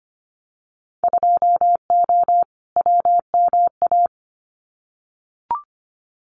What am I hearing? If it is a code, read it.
Morse code, heard as a keyed tone, about 25 wpm, 700 Hz. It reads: 2O WMA
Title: Winter Field Day Exchanges Encore 25wpm